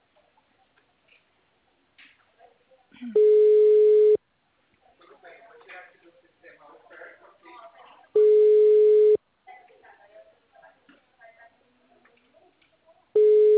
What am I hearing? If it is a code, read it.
Ringback tone (Europe, 425 Hz), with speech.
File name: out- -2019- - - .1820